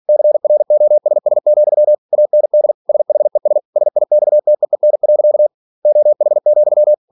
Morse code, heard as a keyed tone, about 46 wpm, 600 Hz. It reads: CROSS, AND SHES SIXTEEN. OH,
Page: Never 47wpm